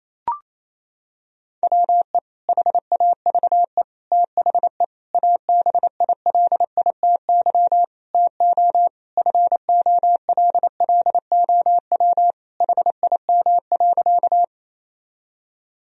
Morse code, heard as a keyed tone, about 28 wpm, 700 Hz. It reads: WE HAVE THE ABILITY TO FOLLOW HIM.
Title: Sentences from Top 1000 Words - Review Part 3 - 28wpm